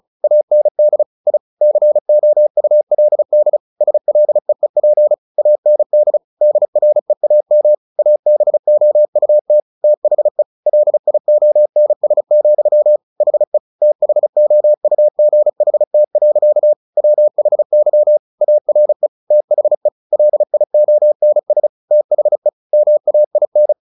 Morse code, heard as a keyed tone, about 34 wpm, 600 Hz. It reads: AND I COULD SLEEP AND DREAM ABOUT THE LIONS, HE THOUGHT. WHY ARE THE LIONS THE MAIN
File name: the old man and the sea / the old man and the sea 1 35wpm